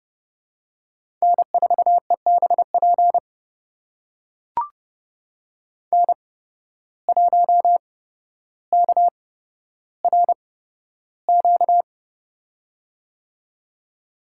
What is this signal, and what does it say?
Morse code, heard as a keyed tone, about 30 wpm, spaced as for 13 wpm, 700 Hz. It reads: N4EBP N1KRQ
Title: USA Call Signs - 5 Characters - 8x Character Spacing - 30wpm